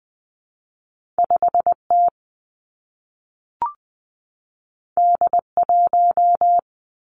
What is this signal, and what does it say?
Morse code, heard as a keyed tone, about 19 wpm, 700 Hz. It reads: 5 T D 1
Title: Two Challenging Letter-Number Combinations 20wpm